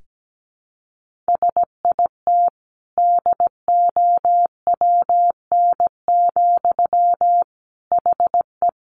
Morse code, heard as a keyed tone, about 17 wpm, 700 Hz. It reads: SIT DOWN, HE
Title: Sentences from Top 1500 Words 17wpm